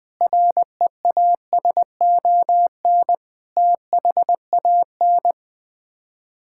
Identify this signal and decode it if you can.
Morse code, heard as a keyed tone, about 20 wpm, 700 Hz. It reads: REASON THAN
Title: Sets of 6 Words - 20wpm